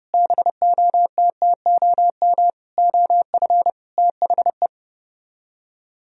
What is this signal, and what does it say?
Morse code, heard as a keyed tone, about 32 wpm, 700 Hz. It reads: BOTTOM OF THE